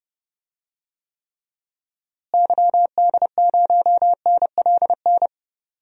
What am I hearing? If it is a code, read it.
Morse code, heard as a keyed tone, about 30 wpm, 700 Hz. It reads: YD0NLN
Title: DXCC-100 Call Signs - 6 Characters - Repeated 3x - 30wpm